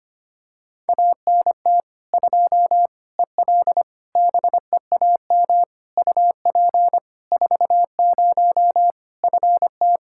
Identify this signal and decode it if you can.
Morse code, heard as a keyed tone, about 25 wpm, 700 Hz. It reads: ANT 2 EL BEAM UP 40 FT
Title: Practice QSO - Exchange 2 - Info - 25wpm